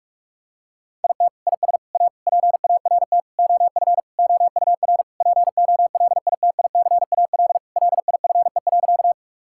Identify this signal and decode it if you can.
Morse code, heard as a keyed tone, about 46 wpm, 700 Hz. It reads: IT IS A PART OF OUR POLITICAL LIFE.
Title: Sentences from Top 1000 Words - Review Part 5 - 45wpm